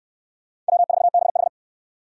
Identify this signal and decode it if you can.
Morse code, heard as a keyed tone, about 68 wpm, 700 Hz. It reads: W4BH